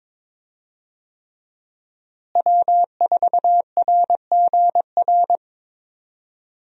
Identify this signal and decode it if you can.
Morse code, heard as a keyed tone, about 22 wpm, 700 Hz. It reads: W4RGR
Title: USA Call Signs - 5 Characters - Repeated 3x - 22wpm